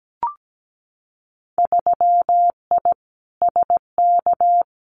Morse code, heard as a keyed tone, about 17 wpm, 700 Hz. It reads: 3I SK
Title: Winter Field Day Exchanges Encore 17wpm